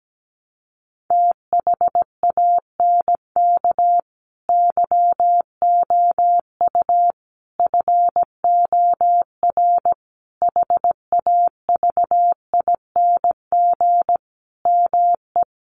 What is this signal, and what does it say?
Morse code, heard as a keyed tone, about 17 wpm, 700 Hz. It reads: THANK YOU FOR HAVING ME
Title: Sets of 5 Words Encore 17wpm